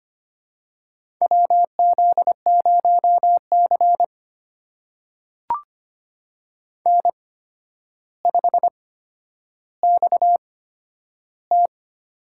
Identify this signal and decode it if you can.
Morse code, heard as a keyed tone, about 25 wpm, spaced as for 11 wpm, 700 Hz. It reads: WZ0C N5XT
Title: USA Call Signs - 4 Characters - 8x Character Spacing - 25wpm